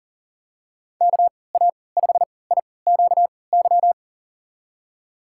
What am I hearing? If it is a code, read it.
Morse code, heard as a keyed tone, about 39 wpm, spaced as for 22 wpm, 700 Hz. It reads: KA5IQY
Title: USA Call Signs - 6 Characters - 3x Character Spacing - 40wpm